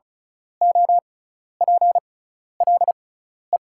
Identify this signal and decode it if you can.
Morse code, heard as a keyed tone, about 36 wpm, spaced as for 12 wpm, 690 Hz. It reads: OPLE
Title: 6x Character Spacing - Top 200 Words - 35wpm